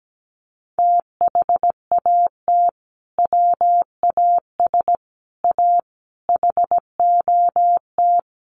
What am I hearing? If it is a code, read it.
Morse code, heard as a keyed tone, about 17 wpm, 700 Hz. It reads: THAT WAS A HOT
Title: Sentences from Top 1100 Words 17wpm